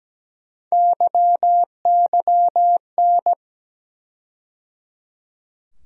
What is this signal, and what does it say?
Morse code, heard as a keyed tone, about 17 wpm, 700 Hz. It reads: YYN
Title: USA Call Sign Postfixes - 3 Letters 17wpm